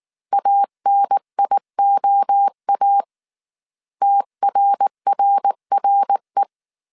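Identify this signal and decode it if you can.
Morse code, heard as a keyed tone, about 19 wpm, 800 Hz. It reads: ANIOA TRRRE
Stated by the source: Morse code dictation at 18 wpm from the IW3SGT QRS Morse code training course, five-character groups of 300 characters.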